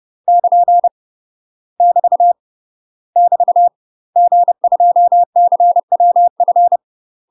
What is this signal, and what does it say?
Morse code, heard as a keyed tone, about 30 wpm, 700 Hz. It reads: <KN> <BT> <BT> G2CWF
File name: UK MASTER QSOs TRAINING / 30WPM / UK MASTER QSOs 0020